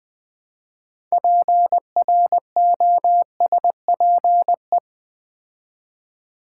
Morse code, heard as a keyed tone, about 20 wpm, 700 Hz. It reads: PROSPE
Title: Build A Word - 3 Letters At A Time - 8-Letter Words - 20wpm